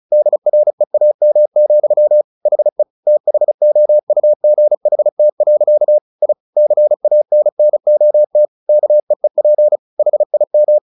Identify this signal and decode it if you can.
Morse code, heard as a keyed tone, about 34 wpm, 600 Hz. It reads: DREAM, HE THOUGHT. I CANNOT KEEP HIM
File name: the old man and the sea 2 35wpm